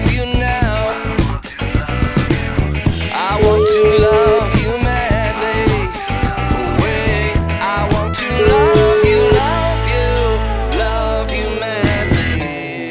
Ringback tone (Europe, 425 Hz), with noise.